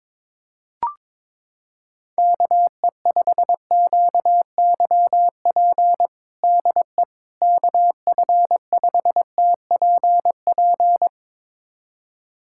Morse code, heard as a keyed tone, about 22 wpm, 700 Hz. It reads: KE5QYP DE KF5TPP
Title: USA Callsign DE Callsign - 6 Characters - 22wpm